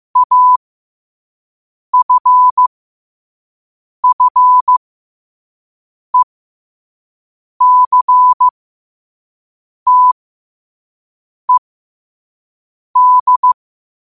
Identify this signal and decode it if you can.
Morse code, heard as a keyed tone, about 15 wpm, spaced as for 5 wpm, 1000 Hz. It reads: AFFECTED
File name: qotdfiles / 5wpm